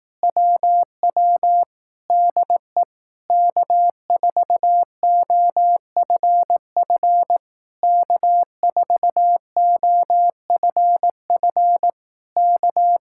Morse code, heard as a keyed tone, about 18 wpm, 700 Hz. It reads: WW DE K4OFF K4OFF K
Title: Speed-Racing Common QSO Phrases Encore - 15wpm